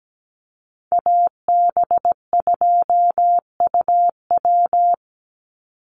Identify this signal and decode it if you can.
Morse code, heard as a keyed tone, about 17 wpm, 700 Hz. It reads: AB2UW
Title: USA Call Signs 17wpm